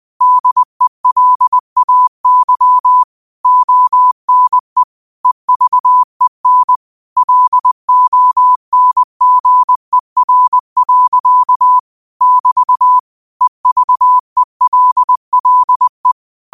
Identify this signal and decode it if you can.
Morse code, heard as a keyed tone, about 20 wpm, 1000 Hz. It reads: DELAY ONE EVEN LONGER. <BT> EVELLE
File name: qotdfiles / 20wpm